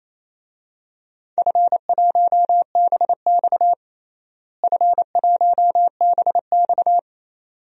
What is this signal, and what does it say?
Morse code, heard as a keyed tone, about 28 wpm, 700 Hz. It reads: F1BX F1BX
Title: DXCC-100 Call Signs - 4 Characters - Repeated 2x - 28wpm